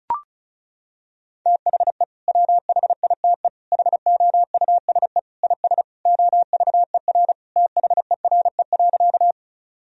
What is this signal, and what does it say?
Morse code, heard as a keyed tone, about 36 wpm, 700 Hz. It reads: THE WHITE HOUSE IS OVER THERE.
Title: Sentences from Top 1000 Words - Review Part 1 - 35wpm